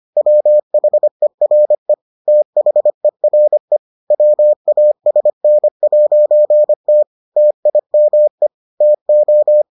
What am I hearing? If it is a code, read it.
Morse code, heard as a keyed tone, about 25 wpm, 600 Hz. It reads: WHERE THERE WASN'T TIME TO